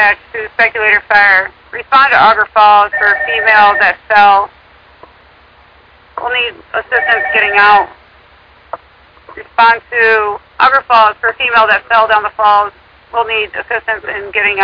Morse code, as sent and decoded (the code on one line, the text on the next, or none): none